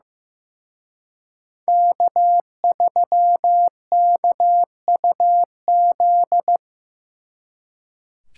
Morse code, heard as a keyed tone, about 18 wpm, 700 Hz. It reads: K3KUZ
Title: Speed Racing - USA 1x3 Call Signs 15wpm